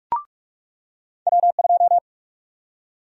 Morse code, heard as a keyed tone, about 46 wpm, 700 Hz. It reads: W2
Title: USA Call Sign Prefixes - 45wpm